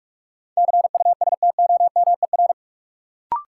Morse code, heard as a keyed tone, about 42 wpm, 700 Hz. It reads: CUSTOMER
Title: Sentences from Top 1100 Words - New Word Drill - 45wpm